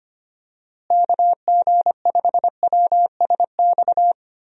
Morse code, heard as a keyed tone, about 25 wpm, 700 Hz. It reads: KG5WSX
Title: USA Call Signs - 6 Characters - Repeated 3x - 25wpm